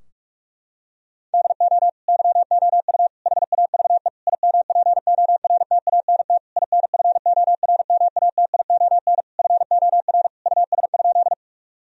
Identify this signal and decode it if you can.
Morse code, heard as a keyed tone, about 41 wpm, 700 Hz. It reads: DO YOU HAVE IMPORTANT INFORMATION FOR US?
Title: Sentences from Top 1000 Words - Review Part 3 - 45wpm